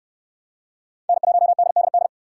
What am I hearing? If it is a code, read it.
Morse code, heard as a keyed tone, about 68 wpm, 700 Hz. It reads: N1DDD